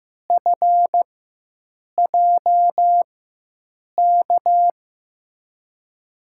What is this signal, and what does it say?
Morse code, heard as a keyed tone, about 15 wpm, spaced as for 7 wpm, 700 Hz. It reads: FJK
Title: USA Call Signs - 4x Character Spacing - 15wpm